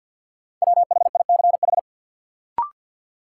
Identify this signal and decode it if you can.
Morse code, heard as a keyed tone, about 48 wpm, 700 Hz. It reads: WHICH
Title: Sentences from Top 100 Words - New Word Drill - 50wpm